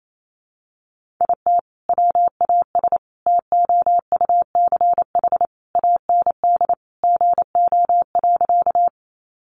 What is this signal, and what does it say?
Morse code, heard as a keyed tone, about 28 wpm, 700 Hz. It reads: IT WAS TOUCH AND GO.
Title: Sentences from Top 1200 Words 28wpm